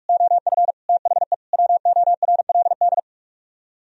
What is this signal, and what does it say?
Morse code, heard as a keyed tone, about 49 wpm, 700 Hz. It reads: OF THE WORLD